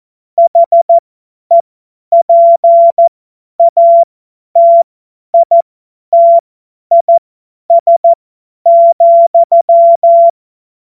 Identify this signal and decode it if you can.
Morse code, heard as a keyed tone, about 14 wpm, spaced as for 10 wpm, 680 Hz. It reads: HEPATITIS,
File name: Morsekurs / Witze / Witz 26 März 2024 50 F70